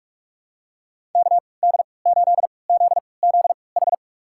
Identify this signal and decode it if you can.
Morse code, heard as a keyed tone, about 46 wpm, spaced as for 26 wpm, 700 Hz. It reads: KD8ZZH
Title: USA Call Signs - 6 Characters - 3x Character Spacing - 45wpm